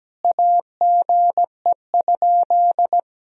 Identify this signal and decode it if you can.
Morse code, heard as a keyed tone, about 17 wpm, 700 Hz. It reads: AGE?
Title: Sentences from Top 1000 Words - Review Part 3 - 17wpm